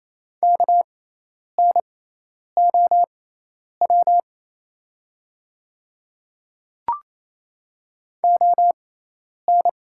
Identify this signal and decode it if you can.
Morse code, heard as a keyed tone, about 27 wpm, spaced as for 10 wpm, 690 Hz. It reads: KNOW ON
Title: 6x Character Spacing - Top 100 Words - 28wpm